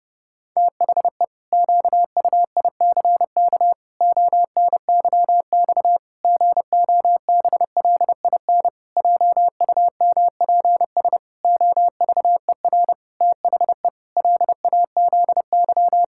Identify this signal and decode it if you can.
Morse code, heard as a keyed tone, about 30 wpm, 700 Hz. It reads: THE QUICK ONYX GOBLIN JUMPS OVER THE LAZY
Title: Pangram Challenge v2 - 30wpm